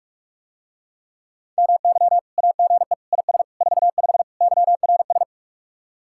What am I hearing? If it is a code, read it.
Morse code, heard as a keyed tone, about 48 wpm, 700 Hz. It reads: MY AGE IS 45 YRS